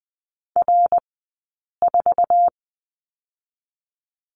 Morse code, heard as a keyed tone, about 20 wpm, 700 Hz. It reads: R 4
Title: 059 Lesson 11d - Letter h - Two Characters 20wpm